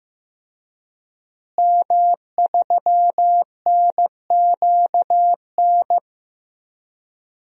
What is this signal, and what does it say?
Morse code, heard as a keyed tone, about 15 wpm, 700 Hz. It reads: M3NQN
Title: DXCC-100 Call Signs - 5 Characters - Repeated 2x - 15wpm